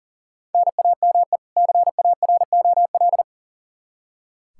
Morse code, heard as a keyed tone, about 40 wpm, 700 Hz. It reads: NAME CAROL
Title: Practice QSO - Exchange 1 - Beginning - 40wpm